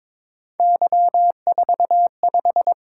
Morse code, heard as a keyed tone, about 23 wpm, 700 Hz. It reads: Y45
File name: practice / 22wpm / Three Letter-Number Combinations - 22wpm